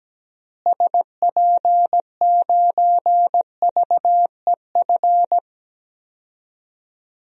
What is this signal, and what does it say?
Morse code, heard as a keyed tone, about 17 wpm, 700 Hz. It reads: SP9VEF
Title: DXCC-100 Call Signs - 6 Characters - Repeated 2x - 17wpm